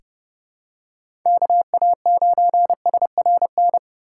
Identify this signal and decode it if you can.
Morse code, heard as a keyed tone, about 30 wpm, 700 Hz. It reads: KA9SRN
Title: USA Call Signs - 6 Characters - 4x Character Spacing - 30wpm